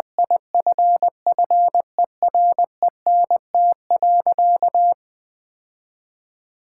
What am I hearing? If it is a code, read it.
Morse code, heard as a keyed tone, about 21 wpm, 700 Hz. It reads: IFFERENT.
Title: Sentences with 4-Word Phrases - Spoken Beginning - 20wpm